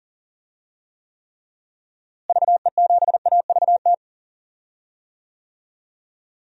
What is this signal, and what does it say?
Morse code, heard as a keyed tone, about 40 wpm, 700 Hz. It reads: VE7AVT
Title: DXCC-100 Call Signs - 6 Characters - Repeated 3x - 40wpm